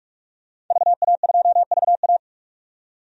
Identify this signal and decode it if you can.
Morse code, heard as a keyed tone, about 42 wpm, 700 Hz. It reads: VA2VA
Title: DXCC-100 Call Signs - 5 Characters - 45wpm